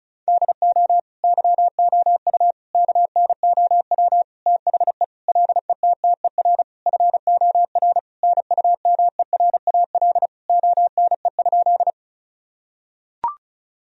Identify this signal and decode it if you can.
Morse code, heard as a keyed tone, about 36 wpm, 700 Hz. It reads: DO YOU KNOW THE LETTER FOR NUMERAL ONE?
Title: Sentences from Top 1000 Words - Review Part 2 - 35wpm